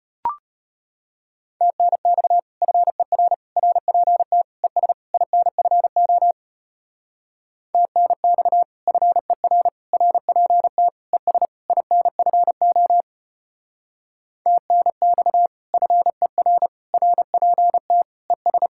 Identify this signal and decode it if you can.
Morse code, heard as a keyed tone, about 33 wpm, 700 Hz. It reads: TNX FER RPT ES INFO TNX FER RPT ES INFO TNX FER RPT ES